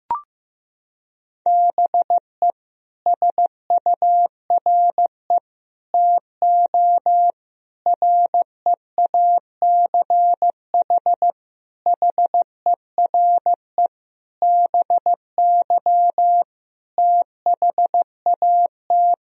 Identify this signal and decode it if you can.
Morse code, heard as a keyed tone, about 15 wpm, 700 Hz. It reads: BE SURE TO REACH HERE BY THAT
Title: Sentences from Top 1000 Words - Review Part 2 - 15wpm